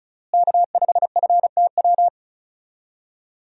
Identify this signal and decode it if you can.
Morse code, heard as a keyed tone, about 36 wpm, 700 Hz. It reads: K5FTW
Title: USA Call Signs - 5 Characters - Repeated 3x - 35wpm